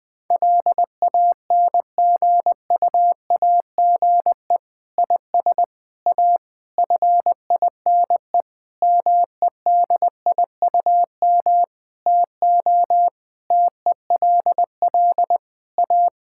Morse code, heard as a keyed tone, about 20 wpm, 700 Hz. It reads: LANGUAGE IS A FINE MEDIUM TO TELL A
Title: Sentences from Top 1000 Words - Review Part 6 - 20wpm